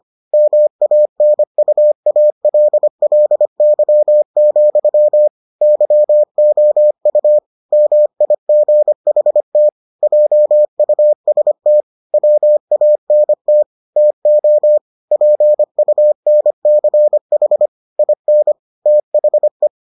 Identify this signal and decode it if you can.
Morse code, heard as a keyed tone, about 25 wpm, 600 Hz. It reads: MANUALLY, YOU MIGHT JUST WANT TO PUNCH IN THE